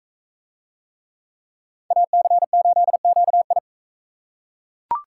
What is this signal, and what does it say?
Morse code, heard as a keyed tone, about 41 wpm, 700 Hz. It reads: AC8QI